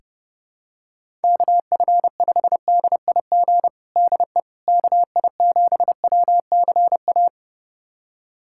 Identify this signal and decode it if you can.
Morse code, heard as a keyed tone, about 32 wpm, 700 Hz. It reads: KF5DIG DE KI7WCA
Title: USA Callsign DE Callsign 30wpm